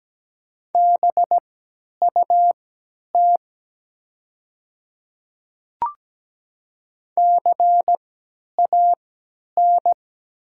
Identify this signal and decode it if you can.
Morse code, heard as a keyed tone, about 17 wpm, spaced as for 10 wpm, 700 Hz. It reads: BUT CAN
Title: 3x Character Spacing - Top 100 Words - 17wpm